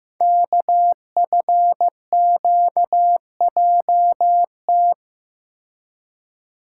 Morse code, heard as a keyed tone, about 15 wpm, 700 Hz. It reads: KFQJT
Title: Code Groups - 5 Letters - 15wpm